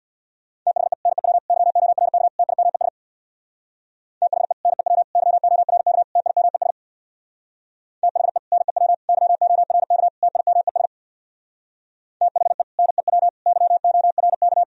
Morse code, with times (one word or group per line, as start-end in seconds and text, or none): 0.7-0.9 THE
1.1-1.4 NEW
1.5-2.3 YORK
2.4-2.9 TIMES
4.2-4.5 THE
4.7-5.0 NEW
5.2-6.0 YORK
6.2-6.7 TIMES
8.0-8.4 THE
8.5-8.9 NEW
9.1-10.1 YORK
10.2-10.9 TIMES
12.2-12.6 THE
12.8-13.3 NEW
13.5-14.6 YORK